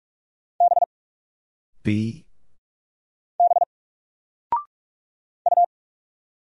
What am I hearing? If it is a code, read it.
Morse code, heard as a keyed tone, about 45 wpm, 700 Hz. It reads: B B U
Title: Single Letter-Number 45wpm